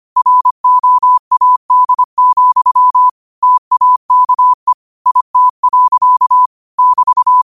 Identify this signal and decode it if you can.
Morse code, heard as a keyed tone, about 25 wpm, 1000 Hz. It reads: ROAD, TAKE IT. <BT>
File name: qotdfiles / 25wpm